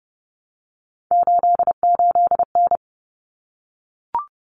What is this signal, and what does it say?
Morse code, heard as a keyed tone, about 30 wpm, 700 Hz. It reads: 88N
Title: Common Cut Numbers - 3-Digit 30wpm